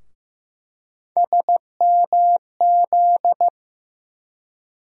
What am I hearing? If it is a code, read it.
Morse code, heard as a keyed tone, about 15 wpm, 700 Hz. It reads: SMZ
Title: USA Call Sign Postfixes - 3 Letters 15wpm